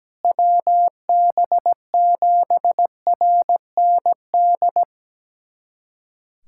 Morse code, heard as a keyed tone, about 17 wpm, 700 Hz. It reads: WB7RND
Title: USA Callsign DE Callsign - 6 Characters - 17wpm